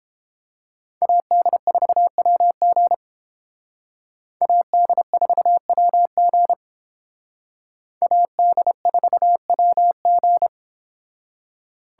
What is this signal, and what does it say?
Morse code, heard as a keyed tone, about 30 wpm, 700 Hz. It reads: AD4WG AD4WG AD4WG